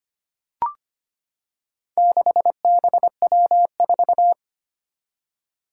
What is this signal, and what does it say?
Morse code, heard as a keyed tone, about 25 wpm, 700 Hz. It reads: 6BW4
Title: Four Challenging Letter-Number Combinations 25wpm